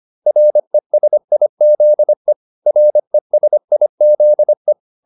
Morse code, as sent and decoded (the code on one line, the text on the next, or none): .-. ... .. --.. / .-. ... .. --.. .
RESIZE RESIZE